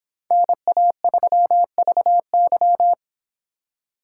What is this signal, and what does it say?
Morse code, heard as a keyed tone, about 26 wpm, 700 Hz. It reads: NA3VY